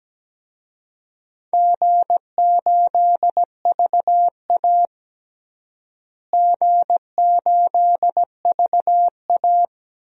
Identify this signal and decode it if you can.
Morse code, heard as a keyed tone, about 17 wpm, 700 Hz. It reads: G8VA G8VA
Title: DXCC-100 Call Signs - 4 Characters - Repeated 3x - 17wpm